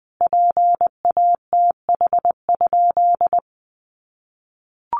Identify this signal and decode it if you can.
Morse code, heard as a keyed tone, about 20 wpm, 700 Hz. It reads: PATH?
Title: Sentences from Top 1400 Words 20wpm